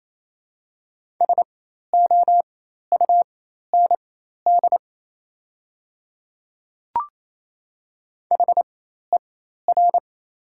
Morse code, heard as a keyed tone, about 28 wpm, spaced as for 13 wpm, 700 Hz. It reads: SOUND HER
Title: 4x Character Spacing - Top 100 Words - 28wpm